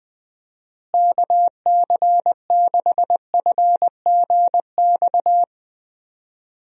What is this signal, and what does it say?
Morse code, heard as a keyed tone, about 20 wpm, 700 Hz. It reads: KC6FGX
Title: USA Call Signs - 6 Characters - Repeated 3x - 20wpm